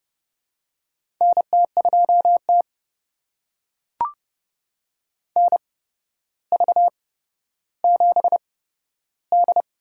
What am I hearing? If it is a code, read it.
Morse code, heard as a keyed tone, about 29 wpm, spaced as for 13 wpm, 700 Hz. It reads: NT2T NV7D
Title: USA Call Signs - 4 Characters - 8x Character Spacing - 30wpm